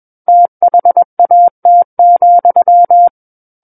Morse code, heard as a keyed tone, about 21 wpm, 700 Hz. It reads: THAT,
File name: Book One / 12-Chapter 12 - 21 WPM